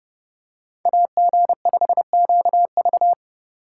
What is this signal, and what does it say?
Morse code, heard as a keyed tone, about 29 wpm, 700 Hz. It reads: AG5QV